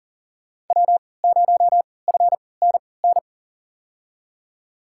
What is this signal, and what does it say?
Morse code, heard as a keyed tone, about 37 wpm, spaced as for 22 wpm, 700 Hz. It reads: W0FNN
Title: USA Call Signs - 5 Characters - 3x Character Spacing - 40wpm